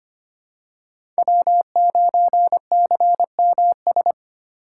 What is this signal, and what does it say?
Morse code, heard as a keyed tone, about 25 wpm, 700 Hz. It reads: W9CMS
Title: USA Call Signs - 5 Characters - 4x Character Spacing - 25wpm